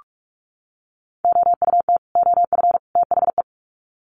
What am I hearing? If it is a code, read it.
Morse code, heard as a keyed tone, about 44 wpm, 700 Hz. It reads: OUT OF THE
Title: Sets of 3 Words 45wpm